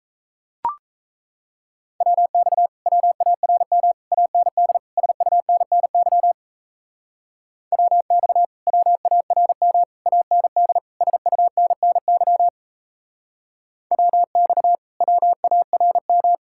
Morse code, heard as a keyed tone, about 36 wpm, 700 Hz. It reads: WX WARM AND SUNNY WX WARM AND SUNNY WX WARM